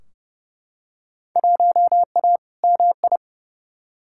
Morse code, heard as a keyed tone, about 30 wpm, 700 Hz. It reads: 1A MI
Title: Field Day Exchanges - 30wpm